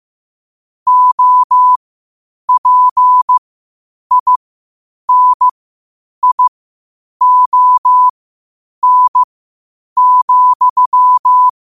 Morse code, heard as a keyed tone, about 15 wpm, spaced as for 8 wpm, 1000 Hz. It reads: OPINION,
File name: qotdfiles / 8wpm